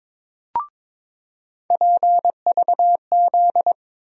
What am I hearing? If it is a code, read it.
Morse code, heard as a keyed tone, about 22 wpm, 700 Hz. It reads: PVZ
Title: USA Call Sign Postfixes - 3 Letters 22wpm